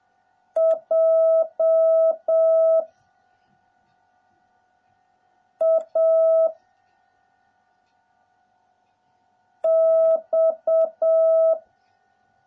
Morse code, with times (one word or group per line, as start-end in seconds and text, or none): 0.6-2.8 J
5.6-6.5 A
9.6-11.5 X